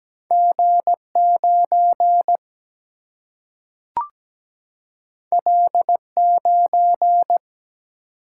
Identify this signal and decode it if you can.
Morse code, heard as a keyed tone, about 17 wpm, 700 Hz. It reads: G9 L9
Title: Single-Letter Single-Number 17wpm